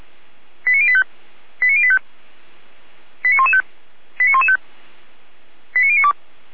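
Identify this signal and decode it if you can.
Five-tone selective call (ZVEI), heard as quick runs of five tones, tones of about 70 ms. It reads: 89085 89085 89115 89115 89092
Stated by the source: ZVEI five-tone selective call received by SDR.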